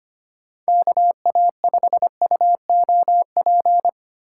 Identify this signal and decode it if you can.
Morse code, heard as a keyed tone, about 25 wpm, 700 Hz. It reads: KA5UOP
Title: USA Call Signs - 6 Characters - 3x Character Spacing - 25wpm